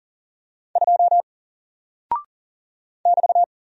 Morse code, heard as a keyed tone, about 40 wpm, 700 Hz. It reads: 2 <BT>
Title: Single Character (Letter:Number:Punctuation:Prosign) 40wpm